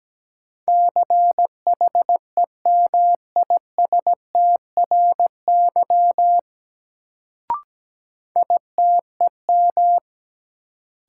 Morse code, heard as a keyed tone, about 17 wpm, 700 Hz. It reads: CHEMISTRY ITEM
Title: Sentences from Top 1000 Words - Review Part 6 - New Word Drill - 17wpm